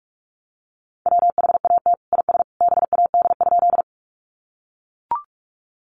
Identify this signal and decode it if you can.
Morse code, heard as a keyed tone, about 44 wpm, 700 Hz. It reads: WHAT IS BAD?
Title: Sentences from Top 700 Words 45wpm